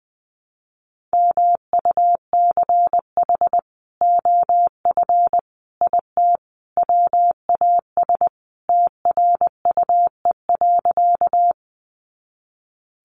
Morse code, heard as a keyed tone, about 20 wpm, 700 Hz. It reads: MUCH OF IT WAS TRUE.
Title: Sentences from Top 400 Words 20wpm